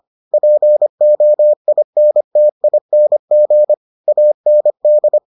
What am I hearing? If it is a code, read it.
Morse code, heard as a keyed tone, about 25 wpm, 600 Hz. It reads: POINTING AND